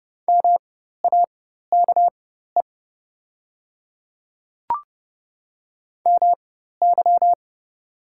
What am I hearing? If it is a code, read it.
Morse code, heard as a keyed tone, about 30 wpm, spaced as for 14 wpm, 700 Hz. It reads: MAKE MY
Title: 4x Character Spacing - Top 200 Words - 30wpm